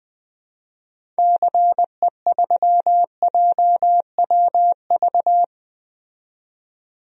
Morse code, heard as a keyed tone, about 20 wpm, 700 Hz. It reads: CE3JWV
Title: DXCC-100 Call Signs - 6 Characters - 20wpm